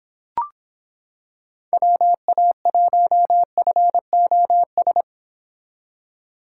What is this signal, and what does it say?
Morse code, heard as a keyed tone, about 26 wpm, 700 Hz. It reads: WA1FOS